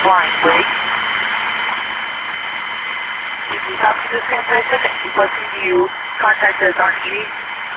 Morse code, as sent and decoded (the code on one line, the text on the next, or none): none